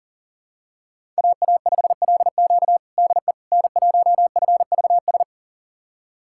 Morse code, heard as a keyed tone, about 37 wpm, 700 Hz. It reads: AA5LQ DE N1FVS
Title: USA Callsign DE Callsign - 5 Characters - 40wpm